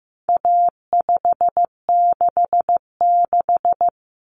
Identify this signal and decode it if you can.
Morse code, heard as a keyed tone, about 15 wpm, 700 Hz. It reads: A566
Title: Common Cut Numbers - 4-Digit 15wpm